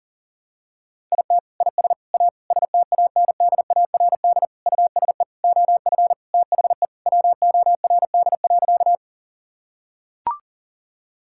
Morse code, heard as a keyed tone, about 40 wpm, 700 Hz. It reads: IT IS A STANDARD USE OF THE WORD.
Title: Sentences from Top 1000 Words - Review Part 6 - 40wpm